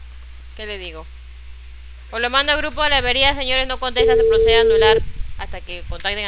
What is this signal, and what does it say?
Ringback tone (China, 450 Hz), with speech.